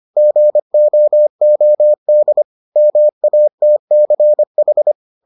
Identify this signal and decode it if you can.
Morse code, heard as a keyed tone, about 25 wpm, 600 Hz. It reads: GOOD MATCH